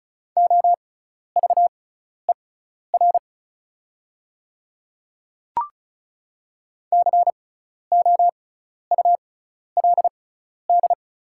Morse code, heard as a keyed tone, about 36 wpm, spaced as for 12 wpm, 700 Hz. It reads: OVER COULD
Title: 6x Character Spacing - Top 100 Words - 35wpm